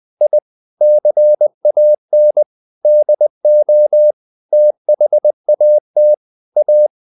Morse code, heard as a keyed tone, about 20 wpm, 600 Hz. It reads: I CAN DO THAT A
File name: the old man and the sea part 2 20wpm